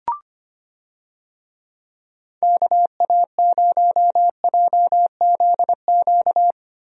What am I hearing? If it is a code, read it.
Morse code, heard as a keyed tone, about 25 wpm, 700 Hz. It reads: KA0JZQ
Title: USA Call Signs - 6 Characters - Repeated 3x - 25wpm